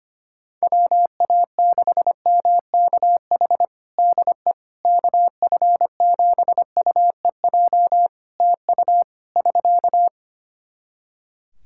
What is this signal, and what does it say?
Morse code, heard as a keyed tone, about 25 wpm, 700 Hz. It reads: WA6MKH DE KF7UEJ TU <SK>
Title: K1USN SST QSO Practice 25wpm